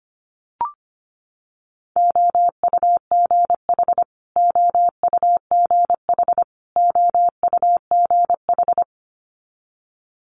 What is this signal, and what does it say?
Morse code, heard as a keyed tone, about 25 wpm, 700 Hz. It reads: OUGH OUGH OUGH
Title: Multi-letter Phonemes Repeated 3x 25wpm